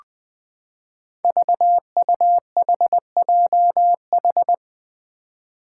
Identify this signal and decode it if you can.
Morse code, heard as a keyed tone, about 20 wpm, 700 Hz. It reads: VUHJH
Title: Five Challenging Letter-Number Combinations 20wpm